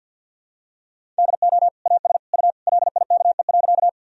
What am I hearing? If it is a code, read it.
Morse code, heard as a keyed tone, about 46 wpm, 700 Hz. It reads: DO AS U LIKE.
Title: Sentences from Top 100 Words and Abbreviations 50wpm